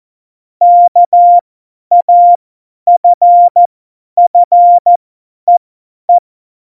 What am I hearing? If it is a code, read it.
Morse code, heard as a keyed tone, about 14 wpm, spaced as for 10 wpm, 700 Hz. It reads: KAFFEE
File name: Morsekurs / Witze / Witz 07. Januar 2020, 50 F70